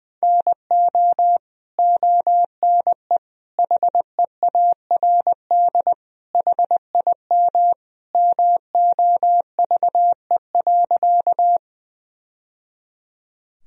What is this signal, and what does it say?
Morse code, heard as a keyed tone, about 20 wpm, 700 Hz. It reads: NO ONE HEARD HIM MOVE.
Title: Sentences from Top 1000 Words - Review Part 1 - 20wpm